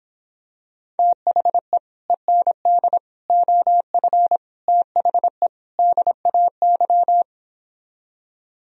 Morse code, heard as a keyed tone, about 26 wpm, 700 Hz. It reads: THE END OF THE DAY